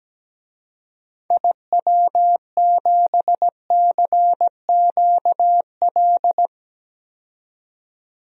Morse code, heard as a keyed tone, about 17 wpm, 700 Hz. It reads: IW7CQL
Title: DXCC-100 Call Signs - 6 Characters - 17wpm